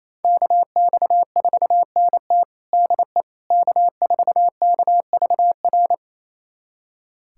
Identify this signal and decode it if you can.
Morse code, heard as a keyed tone, about 27 wpm, 700 Hz. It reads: KX4NT DE K4KVR
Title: USA Callsign DE Callsign - 5 Characters - 28wpm